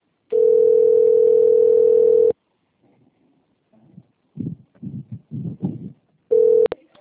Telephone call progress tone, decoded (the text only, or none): ringback tone